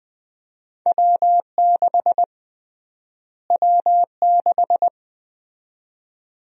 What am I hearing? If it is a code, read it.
Morse code, heard as a keyed tone, about 20 wpm, 700 Hz. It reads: W6 W6
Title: USA Call Sign Prefixes - Repeated 2x - 20wpm